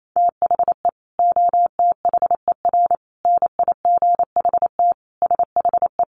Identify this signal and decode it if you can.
Morse code, heard as a keyed tone, about 28 wpm, 700 Hz. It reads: THE OTHER NIGHT SHE
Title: Sentences from Top 300 Words 28wpm